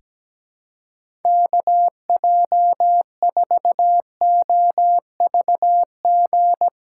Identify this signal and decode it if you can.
Morse code, heard as a keyed tone, about 17 wpm, 700 Hz. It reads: KJ4OVG
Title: USA Call Signs - 6 Characters - 2x Character Spacing - 17wpm